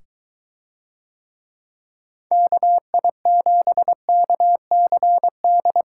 Morse code, heard as a keyed tone, about 22 wpm, 700 Hz. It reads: KI7KCD